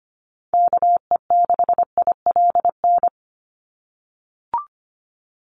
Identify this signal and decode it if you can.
Morse code, heard as a keyed tone, about 25 wpm, 700 Hz. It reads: KE6ILN
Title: USA Call Signs 25wpm